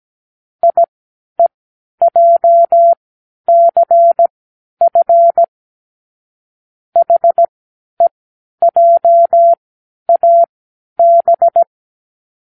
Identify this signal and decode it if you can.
Morse code, heard as a keyed tone, about 17 wpm, spaced as for 11 wpm, 680 Hz. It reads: IEJCF HEJAB
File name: k7qo / Lesson15(A-J)